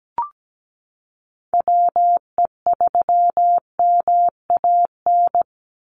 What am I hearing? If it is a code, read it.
Morse code, heard as a keyed tone, about 17 wpm, 700 Hz. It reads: WE3MAN
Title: USA Call Signs 17wpm